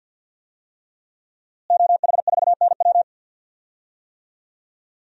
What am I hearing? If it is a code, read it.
Morse code, heard as a keyed tone, about 46 wpm, 700 Hz. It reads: OH4NW
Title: DXCC-100 Call Signs - 5 Characters - Repeated 3x - 50wpm